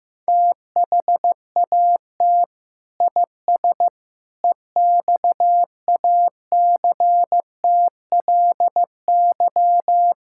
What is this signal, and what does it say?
Morse code, heard as a keyed tone, about 16 wpm, 700 Hz. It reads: THAT IS EXACTLY
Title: Sentences from Top 1000 Words - Review Part 6 - 15wpm